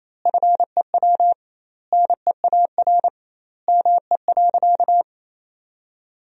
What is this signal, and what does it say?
Morse code, heard as a keyed tone, about 28 wpm, 700 Hz. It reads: FEW NEAR ME.
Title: 2x Word-Spacing - Sentences from Top 300 Words - 28wpm